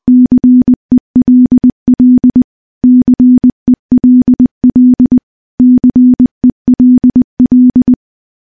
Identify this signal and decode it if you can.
Morse code, heard as a keyed tone, about 21 wpm, 250 Hz. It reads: CELL CELL CELL
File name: morse / morse-training-20WPM